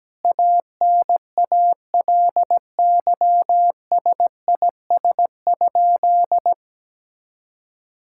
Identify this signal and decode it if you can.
Morse code, heard as a keyed tone, about 17 wpm, 700 Hz. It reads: ANALYSIS?
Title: Sentences from Top 1000 Words - Review Part 5 - 17wpm